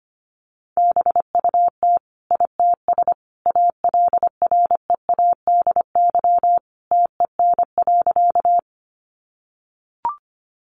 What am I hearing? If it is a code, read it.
Morse code, heard as a keyed tone, about 25 wpm, 700 Hz. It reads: BUT ITS ALREADY TEN.
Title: Sentences from Top 700 Words 25wpm